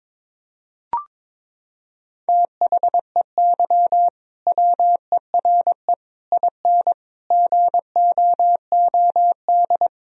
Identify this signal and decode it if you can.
Morse code, heard as a keyed tone, about 22 wpm, 700 Hz. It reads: THEY WERE IN GOOD
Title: Sentences from Top 1000 Words - Review Part 3 - 22wpm